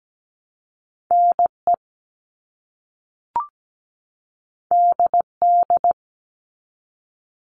Morse code, heard as a keyed tone, about 17 wpm, 700 Hz. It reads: NE DD
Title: All Cut Numbers - 2-Digit 17wpm